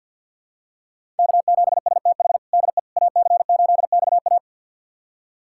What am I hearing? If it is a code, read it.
Morse code, heard as a keyed tone, about 51 wpm, 700 Hz. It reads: K7STH DE AC8XA
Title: USA Callsign DE Callsign - 5 Characters - 50wpm